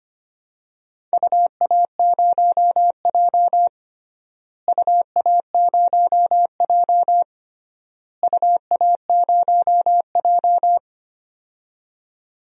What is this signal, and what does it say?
Morse code, heard as a keyed tone, about 25 wpm, 700 Hz. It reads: UA0J UA0J UA0J
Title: DXCC-100 Call Signs - 4 Characters - Repeated 3x - 25wpm